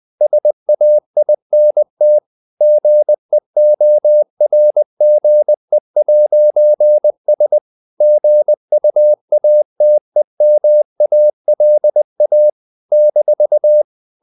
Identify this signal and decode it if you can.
Morse code, heard as a keyed tone, about 20 wpm, 600 Hz. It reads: SAINT GEORGE'S GUATEMALA -
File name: CW / Practice / NationalCapitals20